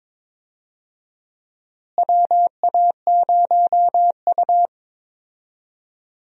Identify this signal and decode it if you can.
Morse code, heard as a keyed tone, about 22 wpm, 700 Hz. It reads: WA0U
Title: USA Call Signs - 4 Characters - Repeated 3x - 22wpm